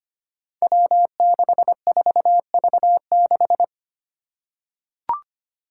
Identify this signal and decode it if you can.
Morse code, heard as a keyed tone, about 25 wpm, 700 Hz. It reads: W64V6
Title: Five Challenging Letter-Number Combinations 25wpm